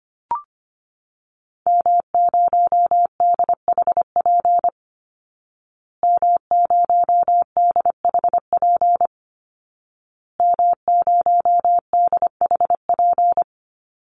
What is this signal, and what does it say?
Morse code, heard as a keyed tone, about 25 wpm, 700 Hz. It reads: M0DHP M0DHP M0DHP
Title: Call Signs Repeated 2x - 25wpm